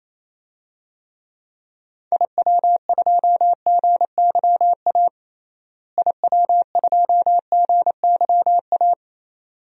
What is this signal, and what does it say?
Morse code, heard as a keyed tone, about 28 wpm, 700 Hz. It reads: IW2GYA IW2GYA
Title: DXCC-100 Call Signs - 6 Characters - Repeated 2x - 28wpm